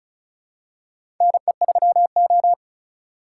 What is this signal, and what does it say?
Morse code, heard as a keyed tone, about 34 wpm, 700 Hz. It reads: NE3O
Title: USA Call Signs - 4 Characters - 4x Character Spacing - 35wpm